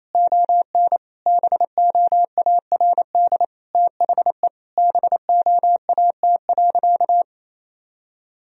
Morse code, heard as a keyed tone, about 28 wpm, 700 Hz. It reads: ON BOARD THE BOAT.
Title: Sentences from Top 1000 Words - Review Part 4 - 28wpm